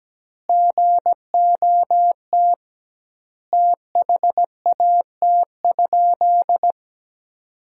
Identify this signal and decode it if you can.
Morse code, heard as a keyed tone, about 17 wpm, 700 Hz. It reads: GOT THAT?
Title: 2x Word-Spacing - Sentences from Top 300 Words - 17wpm